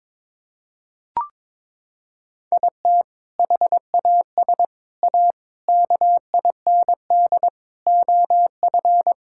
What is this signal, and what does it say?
Morse code, heard as a keyed tone, about 22 wpm, 700 Hz. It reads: IT HAS A KIND OF
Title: Sentences from Top 1000 Words - Review Part 5 - 22wpm